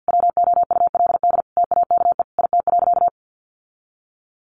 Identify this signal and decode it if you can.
Morse code, heard as a keyed tone, about 50 wpm, 700 Hz. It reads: WOULD TAKE IT.
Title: Sentences from Top 700 Words 50wpm